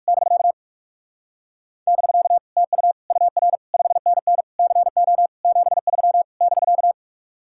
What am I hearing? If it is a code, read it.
Morse code, heard as a keyed tone, about 60 wpm, spaced as for 52 wpm, 700 Hz. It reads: <BK> <BK> TU UR 5NN CO 73 <BK>
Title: Speed-Racing Common QSO Phrases Encore - 45wpm